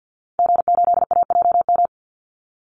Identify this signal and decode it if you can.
Morse code, heard as a keyed tone, about 50 wpm, 700 Hz. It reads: G8AJM
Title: Call Signs Repeated 2x - 50wpm